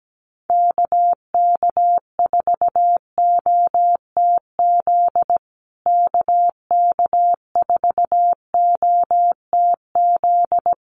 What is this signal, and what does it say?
Morse code, heard as a keyed tone, about 17 wpm, 700 Hz. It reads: KK4OTZ KK4OTZ
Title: USA Call Signs Repeated 3x 17wpm